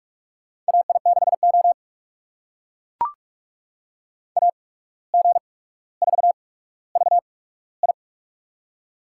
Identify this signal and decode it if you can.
Morse code, heard as a keyed tone, about 44 wpm, spaced as for 18 wpm, 700 Hz. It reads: AI6O AG4VI
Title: USA Call Signs - 8x Character Spacing - 45wpm